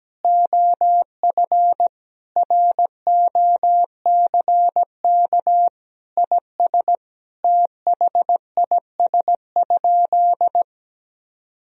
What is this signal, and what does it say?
Morse code, heard as a keyed tone, about 17 wpm, 700 Hz. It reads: OF ROCK IS THIS?
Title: Sentences from Top 1000 Words - Review Part 3 - 17wpm